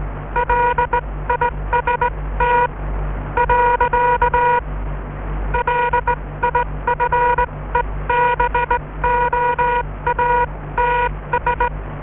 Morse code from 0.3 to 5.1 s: LIST .